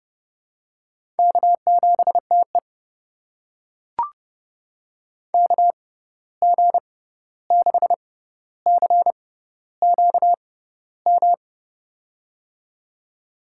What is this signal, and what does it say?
Morse code, heard as a keyed tone, about 31 wpm, spaced as for 14 wpm, 700 Hz. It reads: K7TE KG6CQM
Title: USA Call Signs - 6x Character Spacing - 30wpm